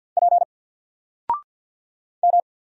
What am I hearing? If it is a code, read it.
Morse code, heard as a keyed tone, about 50 wpm, 700 Hz. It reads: P M
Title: Single Letter-Number 50wpm